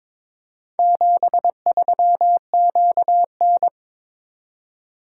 Morse code, heard as a keyed tone, about 22 wpm, 700 Hz. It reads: 73QN
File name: practice / 22wpm / Four Letter-Number Combinations - 22wpm